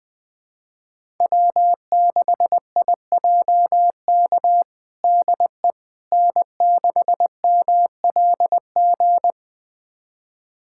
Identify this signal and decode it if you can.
Morse code, heard as a keyed tone, about 20 wpm, 700 Hz. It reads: W6IJK DE N6MLG
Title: USA Callsign DE Callsign - 5 Characters - 20wpm